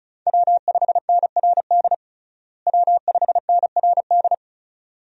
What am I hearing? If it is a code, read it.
Morse code, heard as a keyed tone, about 34 wpm, 700 Hz. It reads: W5NRD W5NRD
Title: USA Call Signs - 5 Characters - Repeated 2x - 35wpm